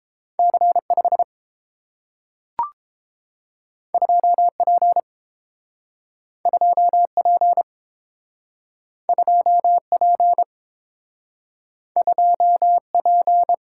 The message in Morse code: C5 2P 2P 2P 2P

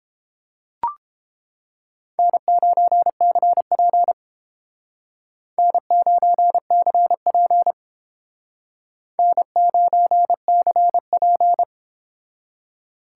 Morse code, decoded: N9CP N9CP N9CP

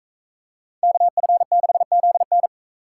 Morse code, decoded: KF6ZN

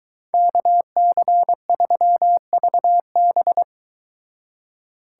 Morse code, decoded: KC3VB

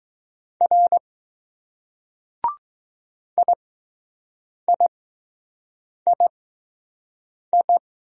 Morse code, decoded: R IIII

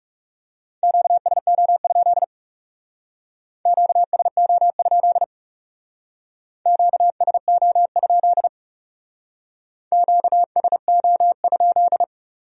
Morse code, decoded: QSO? QSO? QSO? QSO?

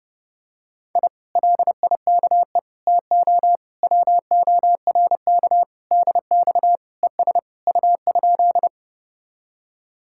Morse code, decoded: I LIKE TO WORK DX ES U?

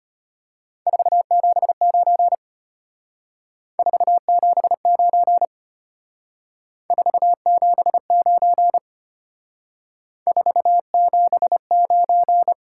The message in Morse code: 479 479 479 479

